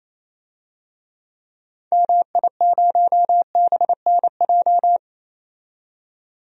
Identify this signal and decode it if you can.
Morse code, heard as a keyed tone, about 28 wpm, 700 Hz. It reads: MI0BNJ
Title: DXCC-100 Call Signs - 6 Characters - 28wpm